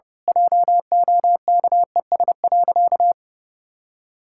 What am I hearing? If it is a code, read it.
Morse code, heard as a keyed tone, about 30 wpm, 700 Hz. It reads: JOKES.
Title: Pangram Challenge v2 - 30wpm